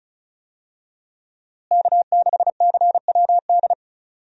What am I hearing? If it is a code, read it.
Morse code, heard as a keyed tone, about 34 wpm, 700 Hz. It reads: K6CWD
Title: USA Call Signs - 5 Characters - Repeated 2x - 35wpm